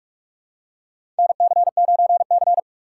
Morse code, heard as a keyed tone, about 46 wpm, 700 Hz. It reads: NC9C
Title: USA Call Signs - 4 Characters - 4x Character Spacing - 45wpm